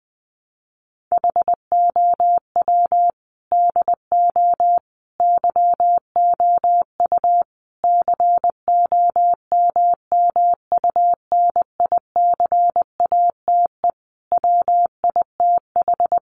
Morse code, heard as a keyed tone, about 20 wpm, 700 Hz. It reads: HOW DO YOU COMMUNICATE WITH
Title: Sentences from Top 1300 Words 20wpm